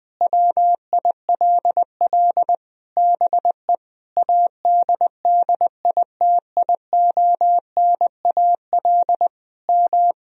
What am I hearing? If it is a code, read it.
Morse code, heard as a keyed tone, about 21 wpm, 700 Hz. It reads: WILL BE ADDITIONAL M
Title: Sentences from Top 1000 Words - Review Part 4 - 20wpm